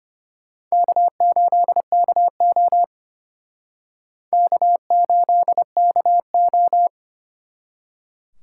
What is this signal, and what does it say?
Morse code, heard as a keyed tone, about 32 wpm, 700 Hz. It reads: K8KO K8KO